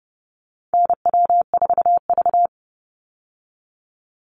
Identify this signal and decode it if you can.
Morse code, heard as a keyed tone, about 30 wpm, 700 Hz. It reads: NW4V
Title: Call Signs Repeated 3x - 30wpm